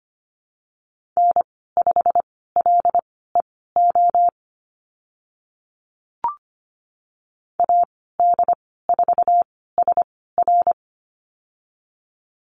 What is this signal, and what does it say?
Morse code, heard as a keyed tone, about 25 wpm, spaced as for 16 wpm, 700 Hz. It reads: N5LEO AD4SR
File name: Lesson 15g - Number 5 - Call Signs 25wpm